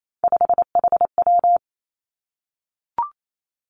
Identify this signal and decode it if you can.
Morse code, heard as a keyed tone, about 28 wpm, 700 Hz. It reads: 5HW
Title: Three Challenging Letter-Number Combinations 28wpm